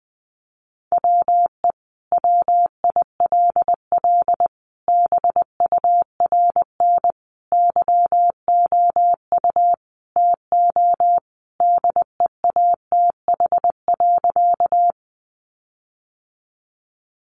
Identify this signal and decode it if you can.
Morse code, heard as a keyed tone, about 20 wpm, 700 Hz. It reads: WE WILL BURN YOU TO DEATH.
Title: Sentences from Top 1300 Words 20wpm